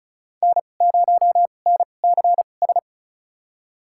Morse code, heard as a keyed tone, about 34 wpm, spaced as for 25 wpm, 700 Hz. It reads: N0NCS
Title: USA Call Signs - 5 Characters - 2x Character Spacing - 35wpm